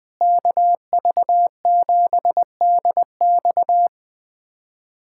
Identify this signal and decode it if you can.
Morse code, heard as a keyed tone, about 20 wpm, 700 Hz. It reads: KV7DX
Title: K1USN SST QSO Practice - 20wpm